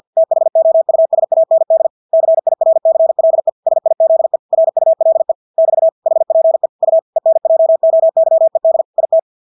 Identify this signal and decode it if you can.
Morse code, heard as a keyed tone, about 50 wpm, 650 Hz. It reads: THOUSAND CIRCLE SIZE RULE <BT> HPE U ENJOYED IT
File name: Engelske ord 50 wpm